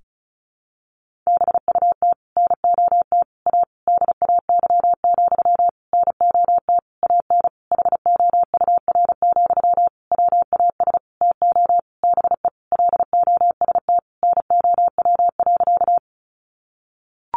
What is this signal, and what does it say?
Morse code, heard as a keyed tone, about 34 wpm, 700 Hz. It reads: BUT NOT A DAY, NOT AN HOUR, WAS TO BE LOST NOW.
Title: Sentences from Top 1400 Words 35wpm